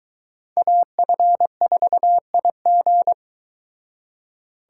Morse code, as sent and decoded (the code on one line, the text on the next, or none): .- ..-. ....- .. --.
AF4IG